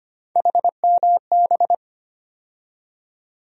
Morse code, heard as a keyed tone, about 25 wpm, 700 Hz. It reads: HMB
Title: USA Call Sign Postfixes - 3 Letters 25wpm